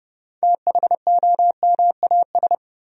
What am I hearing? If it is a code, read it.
Morse code, heard as a keyed tone, about 30 wpm, 700 Hz. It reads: THOMAS
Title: USA Common Names - 30wpm